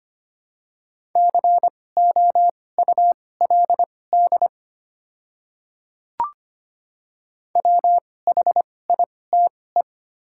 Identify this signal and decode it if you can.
Morse code, heard as a keyed tone, about 25 wpm, spaced as for 18 wpm, 700 Hz. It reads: COULD WHITE
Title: 2x Character Spacing - Top 300 Words - 25wpm